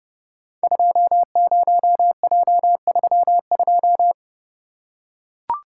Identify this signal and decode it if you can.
Morse code, heard as a keyed tone, about 30 wpm, 700 Hz. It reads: 20J32
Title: Code Groups - 5 Letter:Numbers - 30wpm